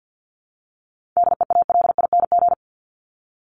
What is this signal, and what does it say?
Morse code, heard as a keyed tone, about 49 wpm, 700 Hz. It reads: DEALING
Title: ING Suffix - Top 100 Words with Base Words 50wpm